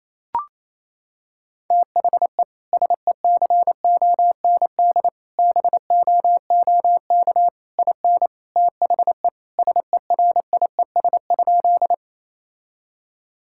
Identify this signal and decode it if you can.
Morse code, heard as a keyed tone, about 27 wpm, 700 Hz. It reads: THE SECOND BOOK IN THE SERIES?
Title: Sentences from Top 1000 Words - Review Part 5 - 28wpm